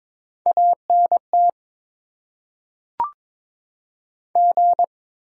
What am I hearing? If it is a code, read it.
Morse code, heard as a keyed tone, about 22 wpm, 700 Hz. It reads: ANT G
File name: Top 90 QSO Elements - 8x Character Spacing - 22wpm